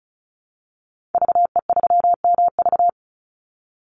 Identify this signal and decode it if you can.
Morse code, heard as a keyed tone, about 34 wpm, 700 Hz. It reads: VE3MV
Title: Call Signs - 35wpm